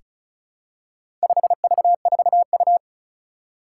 Morse code, heard as a keyed tone, about 36 wpm, 700 Hz. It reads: 5V4U
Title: Four Challenging Letter-Number Combinations 35wpm